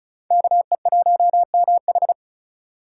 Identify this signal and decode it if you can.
Morse code, heard as a keyed tone, about 45 wpm, 700 Hz. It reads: KE1MH